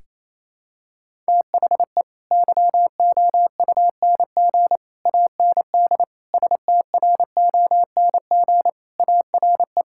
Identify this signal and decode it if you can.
Morse code, heard as a keyed tone, about 28 wpm, 700 Hz. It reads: THE YOUNG AND STRONG ARE
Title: Sentences from Top 1000 Words - Review Part 5 - 28wpm